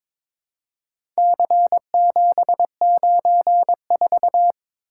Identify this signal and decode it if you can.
Morse code, heard as a keyed tone, about 23 wpm, 700 Hz. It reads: C794
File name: practice / 22wpm / Four Letter-Number Combinations - 22wpm